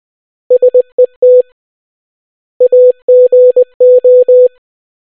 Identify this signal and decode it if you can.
Morse code, heard as a keyed tone, about 20 wpm, 500 Hz. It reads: SET AGO